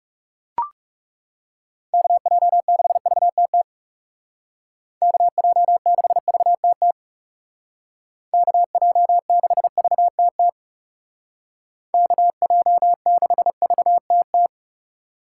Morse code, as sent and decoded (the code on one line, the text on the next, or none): -.- .--- -.... ...- - - / -.- .--- -.... ...- - - / -.- .--- -.... ...- - - / -.- .--- -.... ...- - -
KJ6VTT KJ6VTT KJ6VTT KJ6VTT